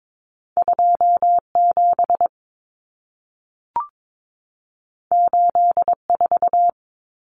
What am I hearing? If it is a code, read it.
Morse code, heard as a keyed tone, about 22 wpm, 700 Hz. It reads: 27 84
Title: Numbers - 2-Digit 22wpm